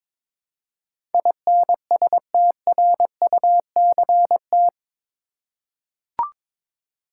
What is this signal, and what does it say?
Morse code, heard as a keyed tone, about 22 wpm, 700 Hz. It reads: INSTRUCT
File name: practice / 22wpm / Eight Letter Words - 22wpm